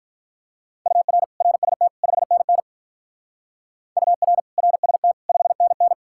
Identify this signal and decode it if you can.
Morse code, heard as a keyed tone, about 44 wpm, 700 Hz. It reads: UR RST 5NN UR RST 5NN